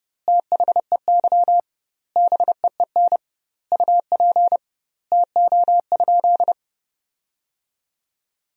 Morse code, heard as a keyed tone, about 32 wpm, 700 Hz. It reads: THEY BEEN UP TO?
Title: 2x Word-Spacing - Sentences from Top 100 Words - 30wpm